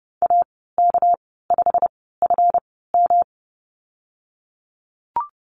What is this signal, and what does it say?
Morse code, heard as a keyed tone, about 30 wpm, spaced as for 17 wpm, 700 Hz. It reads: AK5FM
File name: Lesson 37g - Letter x - Call Signs 30wpm